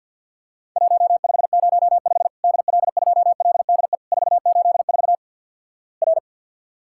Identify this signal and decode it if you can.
Morse code, heard as a keyed tone, about 50 wpm, 700 Hz. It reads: 1505 DL2LDE 484 R
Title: Worked All Europe DX Contest - QTC Exchange with Repeats - 50wpm